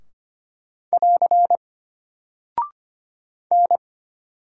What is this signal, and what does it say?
Morse code, heard as a keyed tone, about 25 wpm, 700 Hz. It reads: <AR> N
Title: Single Character (Letter:Number:Punctuation:Prosign) Alt 25wpm